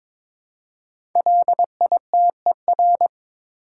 Morse code, heard as a keyed tone, about 23 wpm, 700 Hz. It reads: LITER
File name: 5 Letter Words - First 3 Letters Given - 22wpm